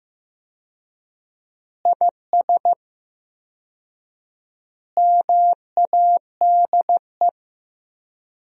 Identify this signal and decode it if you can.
Morse code, heard as a keyed tone, about 15 wpm, 700 Hz. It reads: IS MADE
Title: 4x Word-Spacing - Sentences from Top 500 Words - 15wpm